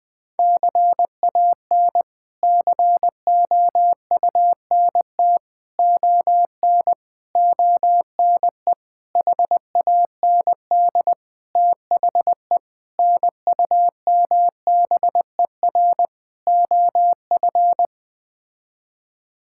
Morse code, heard as a keyed tone, about 20 wpm, 700 Hz. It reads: CAN COUNT ON ONE HAND THE NUMBER OF
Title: Sets of 8 Words - 20wpm